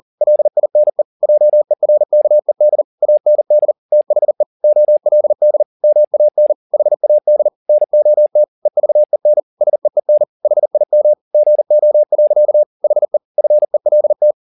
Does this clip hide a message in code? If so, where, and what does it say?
Yes: Morse code from 0.2 to 14.3 s, heard as a keyed tone, about 40 wpm, 600 Hz. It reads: LINE JERKED AND THE OLD MAN HAD NOT EVEN SEEN HIM GO. HE FELT